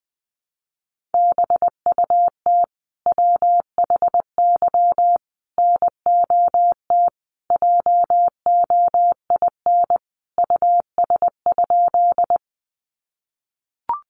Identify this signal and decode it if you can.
Morse code, heard as a keyed tone, about 20 wpm, 700 Hz. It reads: BUT WHY NOT JOIN US?
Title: Sentences from Top 1400 Words 20wpm